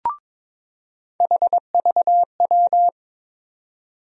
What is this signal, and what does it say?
Morse code, heard as a keyed tone, about 22 wpm, 700 Hz. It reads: HVW
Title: USA Call Sign Postfixes - 3 Letters 22wpm